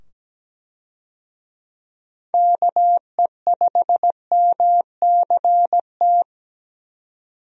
Morse code, heard as a keyed tone, about 17 wpm, 700 Hz. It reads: KE5MCT
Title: USA Call Signs - 6 Characters - 17wpm